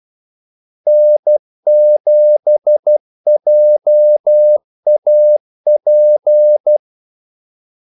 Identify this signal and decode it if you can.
Morse code, heard as a keyed tone, about 12 wpm, 600 Hz. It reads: N7JAP
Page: Kaldesignaler 12 wpm